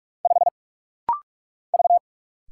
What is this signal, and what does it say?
Morse code, heard as a keyed tone, about 46 wpm, 700 Hz. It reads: 5 V
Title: Single Letter-Number - Rapid-Fire 45wpm